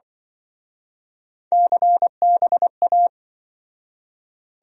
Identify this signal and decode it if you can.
Morse code, heard as a keyed tone, about 25 wpm, 700 Hz. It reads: CBA